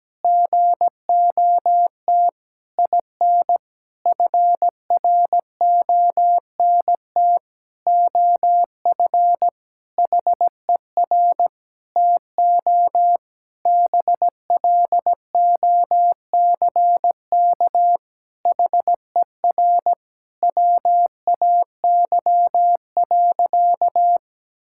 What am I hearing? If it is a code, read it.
Morse code, heard as a keyed tone, about 17 wpm, 700 Hz. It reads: GOT IN FRONT OF HER TO BLOCK HER WAY.
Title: Sentences from Top 1600 Words 17wpm